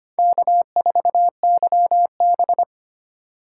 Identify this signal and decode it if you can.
Morse code, heard as a keyed tone, about 25 wpm, 700 Hz. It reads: K4YB
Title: USA Call Signs - 4 Characters - 3x Character Spacing - 25wpm